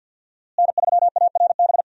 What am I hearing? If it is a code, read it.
Morse code, heard as a keyed tone, about 46 wpm, 700 Hz. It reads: N3ARB
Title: USA Call Signs - 5 Characters - Repeated 2x - 50wpm